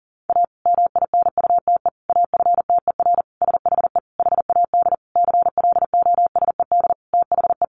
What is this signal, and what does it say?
Morse code, heard as a keyed tone, about 40 wpm, 700 Hz. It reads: A MINUTE AFTER SHE HAD CLOSED THE
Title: Sentences from Top 1400 Words 40wpm